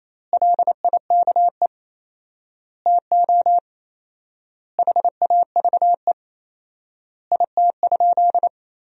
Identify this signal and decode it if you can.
Morse code, heard as a keyed tone, about 28 wpm, 700 Hz. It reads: LIKE TO HAVE IT?
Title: 4x Word-Spacing - Sentences from Top 100 Words - 28wpm